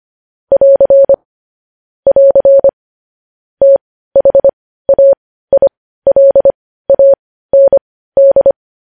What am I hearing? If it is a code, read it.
Morse code, heard as a keyed tone, about 25 wpm, spaced as for 15 wpm, 550 Hz. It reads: <AR> <AR> THAILAND
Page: news headlines in morse code at 15 wpm